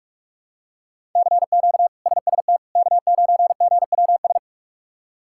Morse code, heard as a keyed tone, about 47 wpm, 700 Hz. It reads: CQ SST K9GWS
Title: K1USN SST QSO Practice - 45wpm